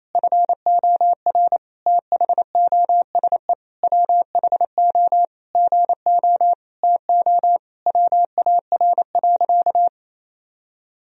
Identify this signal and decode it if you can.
Morse code, heard as a keyed tone, about 28 wpm, 700 Hz. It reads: FOR THOSE WHO GO TO WAR.
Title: Sentences from Top 1000 Words - Review Part 2 - 28wpm